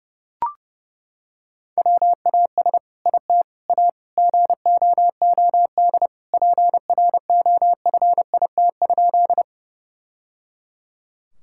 Morse code, heard as a keyed tone, about 30 wpm, 700 Hz. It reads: WAS IT A GOOD PROFIT?
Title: Sentences from Top 1000 Words - Review Part 5 - 30wpm